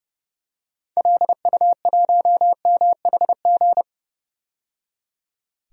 Morse code, heard as a keyed tone, about 32 wpm, 700 Hz. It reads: LU1MHG
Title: DXCC-100 Call Signs - 6 Characters - 30wpm